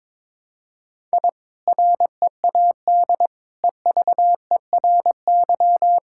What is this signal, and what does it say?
Morse code, heard as a keyed tone, about 22 wpm, 700 Hz. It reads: I READ EVERY
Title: Sentences from Top 1000 Words - Review Part 5 - 22wpm